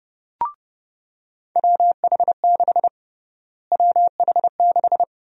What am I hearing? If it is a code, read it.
Morse code, heard as a keyed tone, about 30 wpm, 700 Hz. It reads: WH6 WH6
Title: USA Call Sign Prefixes - Repeated 2x - 30wpm